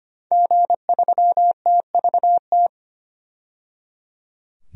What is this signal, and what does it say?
Morse code, heard as a keyed tone, about 25 wpm, 700 Hz. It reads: G3TVT
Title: DXCC-100 Call Signs - 5 Characters - 25wpm